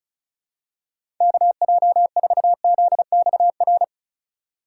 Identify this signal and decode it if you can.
Morse code, heard as a keyed tone, about 34 wpm, 700 Hz. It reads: KJ4ZXR
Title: USA Call Signs - 6 Characters - 4x Character Spacing - 35wpm